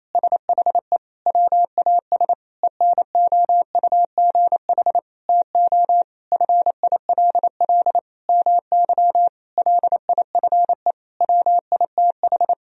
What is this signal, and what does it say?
Morse code, heard as a keyed tone, about 28 wpm, 700 Hz. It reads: SHE WAS ENOUGH TO FILL MY LIFE WITH
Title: Sentences from Top 1000 Words - Review Part 3 - 28wpm